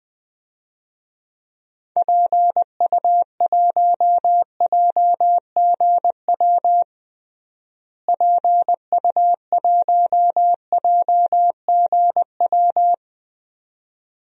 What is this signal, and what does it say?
Morse code, heard as a keyed tone, about 20 wpm, 700 Hz. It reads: PU1JGW PU1JGW
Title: DXCC-100 Call Signs - 6 Characters - Repeated 2x - 20wpm